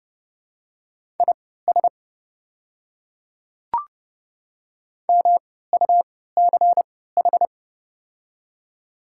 Morse code, heard as a keyed tone, about 31 wpm, spaced as for 17 wpm, 690 Hz. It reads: IS MUCH
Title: 3x Character Spacing - Top 300 Words - 30wpm